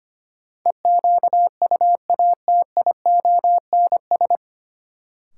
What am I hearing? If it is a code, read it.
Morse code, heard as a keyed tone, about 25 wpm, 700 Hz. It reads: EQUATIONS
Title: Sets of 5 Words - New Word Drill - 25wpm